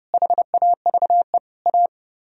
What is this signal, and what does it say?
Morse code, heard as a keyed tone, about 38 wpm, 700 Hz. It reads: HAVE A